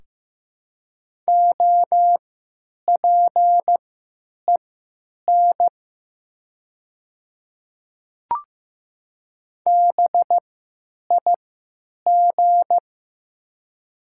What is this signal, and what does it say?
Morse code, heard as a keyed tone, about 15 wpm, spaced as for 9 wpm, 700 Hz. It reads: OPEN BIG
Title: 3x Character Spacing - Top 300 Words - 15wpm